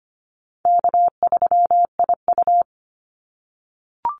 Morse code, heard as a keyed tone, about 25 wpm, 700 Hz. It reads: K3IU
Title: Easy Call Signs - 25wpm